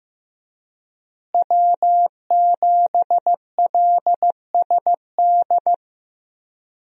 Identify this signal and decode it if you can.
Morse code, heard as a keyed tone, about 15 wpm, 700 Hz. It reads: W7LSD
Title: USA Call Signs - 5 Characters - Repeated 3x - 15wpm